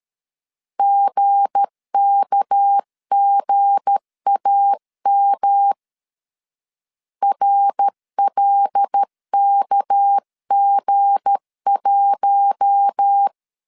Morse code, heard as a keyed tone, about 13 wpm, 800 Hz. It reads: GKGAM RLKG1
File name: LEZIONI / 7 GF9Z / 7 G 12wpm 2055